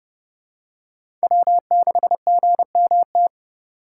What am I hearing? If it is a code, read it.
Morse code, heard as a keyed tone, about 32 wpm, 700 Hz. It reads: W6GMT
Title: USA Call Signs - 2x Character Spacing - 30wpm